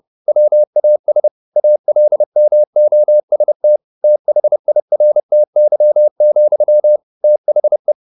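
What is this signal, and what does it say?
Morse code, heard as a keyed tone, about 30 wpm, 600 Hz. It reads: WAS ALMOST THIRTY, THE
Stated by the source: Morse code at 29 wpm, sending Ken Follett's novel Never in English, from CW med Gnister.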